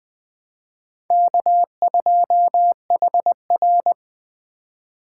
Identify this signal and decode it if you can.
Morse code, heard as a keyed tone, about 24 wpm, 700 Hz. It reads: K2HR
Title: Speed Racing - USA 1x2 Call Signs 20wpm